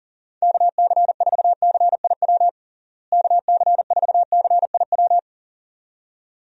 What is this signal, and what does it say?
Morse code, heard as a keyed tone, about 40 wpm, 700 Hz. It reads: KC4CIW KC4CIW
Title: USA Call Signs - 6 Characters - Repeated 2x - 40wpm